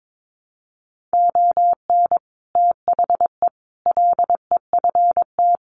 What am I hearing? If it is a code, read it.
Morse code, heard as a keyed tone, about 22 wpm, 700 Hz. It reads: ON THE LEFT
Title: Sets of 5 Words Encore2 22wpm